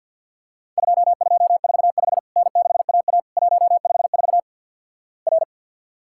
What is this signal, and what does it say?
Morse code, heard as a keyed tone, about 48 wpm, 700 Hz. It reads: 2245 N6AA 154 R
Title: Worked All Europe DX Contest - QTC Exchange - 50wpm